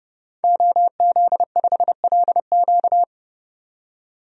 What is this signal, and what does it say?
Morse code, heard as a keyed tone, about 32 wpm, 700 Hz. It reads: OZ5LQ
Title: DXCC-100 Call Signs - 5 Characters - 30wpm